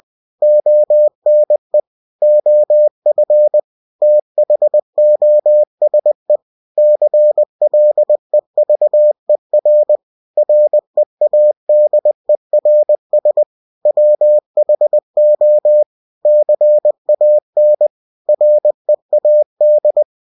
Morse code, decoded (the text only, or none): ONE OF THOSE CLEVER READERS WHO CAN READ